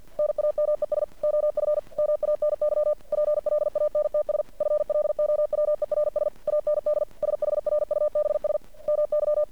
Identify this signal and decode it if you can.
Morse code, heard as a keyed tone, about 51 wpm, 600 Hz. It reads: NAMES OF MANY PLANTS FLOWERS AND SHRUBS MY
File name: archived / CRSnet